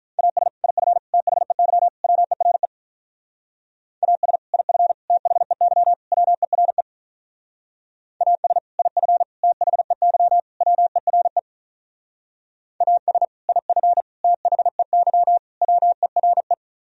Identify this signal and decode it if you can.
Morse code, heard as a keyed tone, about 44 wpm, 700 Hz. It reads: AS IF THEY WERE AS IF THEY WERE AS IF THEY WERE AS IF THEY WERE